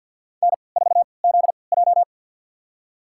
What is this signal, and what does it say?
Morse code, heard as a keyed tone, about 52 wpm, spaced as for 29 wpm, 700 Hz. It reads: N4ZJ
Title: USA Call Signs - 4 Characters - 3x Character Spacing - 50wpm